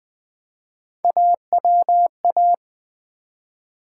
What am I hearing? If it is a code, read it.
Morse code, heard as a keyed tone, about 20 wpm, 700 Hz. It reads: AWA
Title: Build A Word - 1 Letter At A Time - 3 to 5-Letter Words - 20wpm